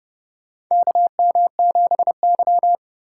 Morse code, heard as a keyed tone, about 30 wpm, 700 Hz. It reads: KM7Y
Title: USA Call Signs - 4 Characters - 2x Character Spacing - 30wpm